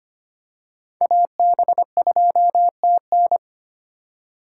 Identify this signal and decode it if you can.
Morse code, heard as a keyed tone, about 25 wpm, 700 Hz. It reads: AB2TN
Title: USA Call Signs - 5 Characters - 4x Character Spacing - 25wpm